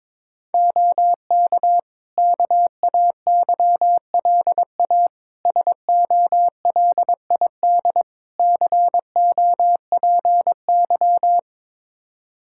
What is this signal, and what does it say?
Morse code, heard as a keyed tone, about 22 wpm, 700 Hz. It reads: OK KAYLA SOLID COPY
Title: K1USN SST QSO Practice 22wpm